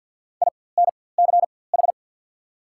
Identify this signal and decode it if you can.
Morse code, heard as a keyed tone, about 54 wpm, spaced as for 24 wpm, 690 Hz. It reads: INCH
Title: 4x Character Spacing - Top 500 Words - 50wpm